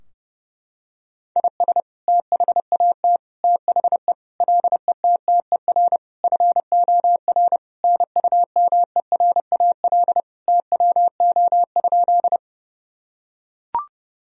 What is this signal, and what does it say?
Morse code, heard as a keyed tone, about 32 wpm, 700 Hz. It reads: IS THAT THE LETTER FOR NUMERAL TWO?
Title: Sentences from Top 1000 Words - Review Part 2 - 30wpm